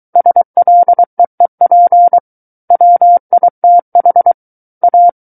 Morse code, heard as a keyed tone, about 23 wpm, 700 Hz. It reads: SLEEP WITH A
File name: Book One / 14-Chapter 14 - 23 WPM